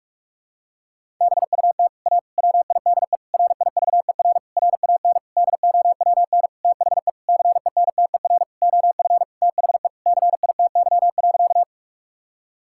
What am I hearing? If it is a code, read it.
Morse code, heard as a keyed tone, about 42 wpm, 700 Hz. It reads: BUT A WIDE RIVER RAN DOWN THE CENTER OF THE CITY.
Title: Sentences from Top 1000 Words - Review Part 6 - 45wpm